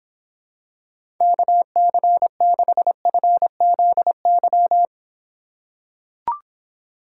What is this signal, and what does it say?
Morse code, heard as a keyed tone, about 26 wpm, 700 Hz. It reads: KC6FZY